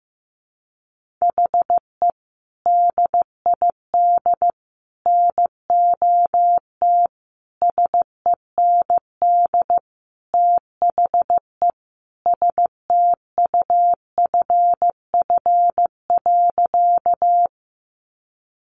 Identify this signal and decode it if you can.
Morse code, heard as a keyed tone, about 15 wpm, 700 Hz. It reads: HE DID NOT SEND THE STUFF.
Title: Sentences from Top 1500 Words 15wpm